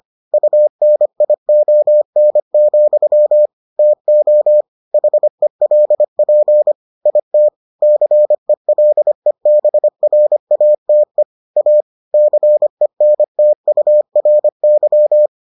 Morse code, heard as a keyed tone, about 25 wpm, 600 Hz. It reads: UNION, TO HELP IT CELEBRATE A CENTURY